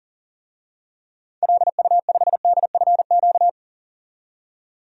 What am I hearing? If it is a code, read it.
Morse code, heard as a keyed tone, about 40 wpm, 700 Hz. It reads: LU5DFQ
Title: DXCC-100 Call Signs - 6 Characters - 40wpm